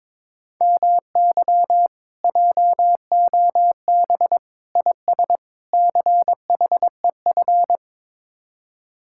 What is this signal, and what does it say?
Morse code, heard as a keyed tone, about 23 wpm, 700 Hz. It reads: MY JOB IS CHEF
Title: Speed-Racing Common QSO Phrases Encore - 22wpm